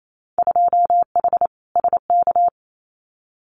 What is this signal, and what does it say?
Morse code, heard as a keyed tone, about 28 wpm, 700 Hz. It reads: 2H SK
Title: Winter Field Day Exchanges Encore 28wpm